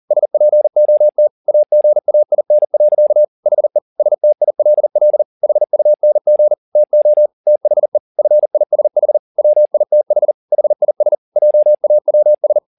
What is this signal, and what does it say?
Morse code, heard as a keyed tone, about 40 wpm, 600 Hz. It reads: SPOT AGAIN. HE STILL HUNG TO THE FISH WITH HIS JAWS
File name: the old man and the sea 2 40wpm